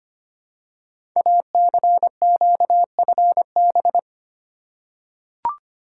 Morse code, heard as a keyed tone, about 25 wpm, 700 Hz. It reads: ACQFB
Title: Code Groups - 5 Letters - 25wpm